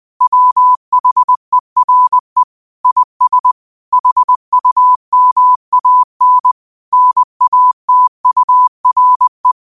Morse code, heard as a keyed tone, about 20 wpm, 1000 Hz. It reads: WHERE IS HUMAN NATURE